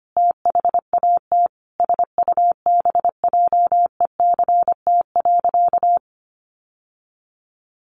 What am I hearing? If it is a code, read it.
Morse code, heard as a keyed tone, about 25 wpm, 700 Hz. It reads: THAT SUBJECT.
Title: Sentences from Top 800 Words 25wpm